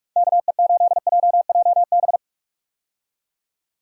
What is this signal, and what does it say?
Morse code, heard as a keyed tone, about 46 wpm, 700 Hz. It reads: KE8JJB
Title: USA Call Signs - 6 Characters - 45wpm